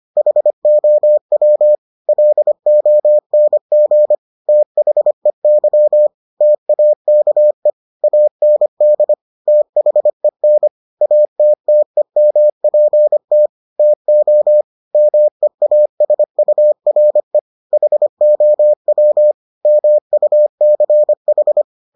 Morse code, heard as a keyed tone, about 25 wpm, 600 Hz. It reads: HOW LONG THEY TAKE AND THEN ATTEMPT TO MEASURE HOW MUCH